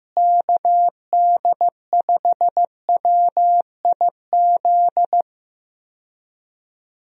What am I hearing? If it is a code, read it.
Morse code, heard as a keyed tone, about 15 wpm, 700 Hz. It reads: KD5WIZ
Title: USA Call Signs - 6 Characters - Repeated 2x - 15wpm